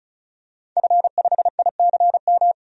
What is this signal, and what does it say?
Morse code, heard as a keyed tone, about 34 wpm, 700 Hz. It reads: F5ICM
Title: DXCC-100 Call Signs - 5 Characters - 35wpm